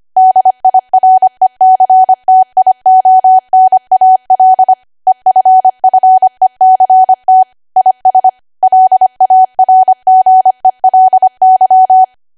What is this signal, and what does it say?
Morse code, heard as a keyed tone, about 25 wpm, 750 Hz. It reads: DIRECTIONAL EFFECT IS LARGELY
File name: Archive / 25 WPM / 25WPM